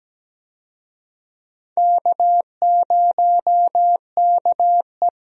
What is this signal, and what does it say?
Morse code, heard as a keyed tone, about 17 wpm, 700 Hz. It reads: K0KE
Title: USA Call Signs - 4 Characters - Repeated 2x - 17wpm